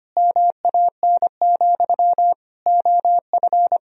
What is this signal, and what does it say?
Morse code, heard as a keyed tone, about 25 wpm, 700 Hz. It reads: MAN, OF
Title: Sentences from Top 1000 Words - Review Part 1 - 25wpm